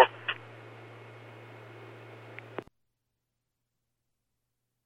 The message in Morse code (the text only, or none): none